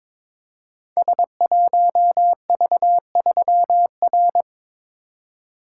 Morse code, heard as a keyed tone, about 22 wpm, 700 Hz. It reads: S1V3R
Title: Code Groups - 5 Letter:Numbers - 22wpm